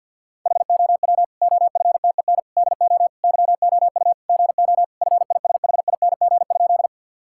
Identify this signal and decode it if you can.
Morse code, heard as a keyed tone, about 49 wpm, 700 Hz. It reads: HOW OFTEN DO YOU GO FISHING?
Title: Sentences from Top 1000 Words - Review Part 5 - 50wpm